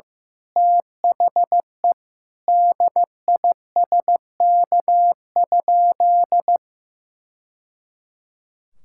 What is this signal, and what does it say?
Morse code, heard as a keyed tone, about 16 wpm, 700 Hz. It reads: THE DISK?
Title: Sentences from Top 1000 Words - Review Part 6 - 15wpm